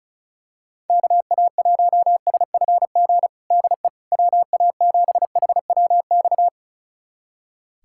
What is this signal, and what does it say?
Morse code, heard as a keyed tone, about 36 wpm, 700 Hz. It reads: KA1SFG DE WA7HWX
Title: USA Callsign DE Callsign 35wpm